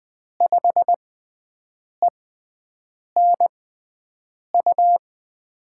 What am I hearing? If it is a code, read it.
Morse code, heard as a keyed tone, about 20 wpm, spaced as for 7 wpm, 700 Hz. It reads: 5ENU
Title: USA Call Signs - 6 Characters - 6x Character Spacing - 20wpm